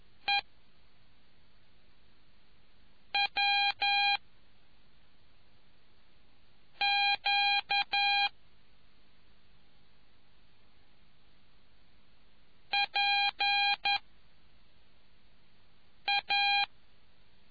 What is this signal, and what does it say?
Morse code, heard as a keyed tone, about 11 wpm, 2400 Hz. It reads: E W Q P A